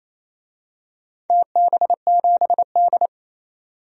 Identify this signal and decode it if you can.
Morse code, heard as a keyed tone, about 27 wpm, 700 Hz. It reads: TB7D
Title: DXCC-100 Call Signs - 4 Characters - 28wpm